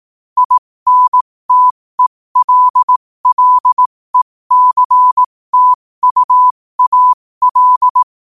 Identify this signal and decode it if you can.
Morse code, heard as a keyed tone, about 18 wpm, spaced as for 15 wpm, 1000 Hz. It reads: INTELLECTUAL
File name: qotdfiles / 15wpm